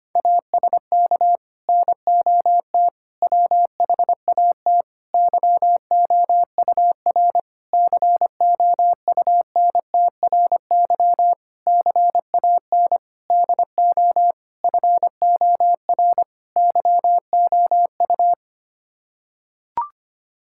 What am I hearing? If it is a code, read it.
Morse code, heard as a keyed tone, about 25 wpm, 700 Hz. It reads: ASK NOT WHAT YOUR COUNTRY CAN DO FOR YOU
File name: Sets of 9 Words - 25wpm